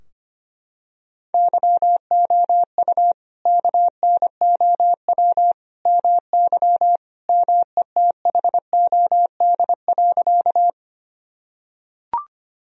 Morse code, heard as a keyed tone, about 25 wpm, 700 Hz. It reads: YOU KNOW MY METHOD.
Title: Sentences from Top 1000 Words - Review Part 3 - 25wpm